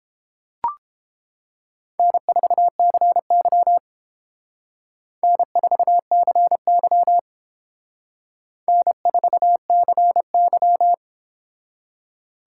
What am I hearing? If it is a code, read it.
Morse code, heard as a keyed tone, about 29 wpm, 700 Hz. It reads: N4CY N4CY N4CY